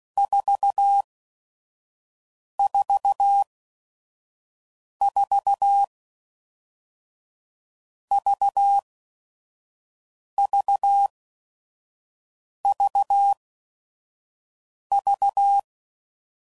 Morse code, heard as a keyed tone, about 16 wpm, 800 Hz. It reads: 4 4 4 V V V V